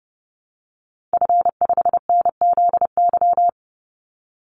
Morse code, heard as a keyed tone, about 30 wpm, 700 Hz. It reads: F5NZY
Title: Call Signs Repeated 3x - 30wpm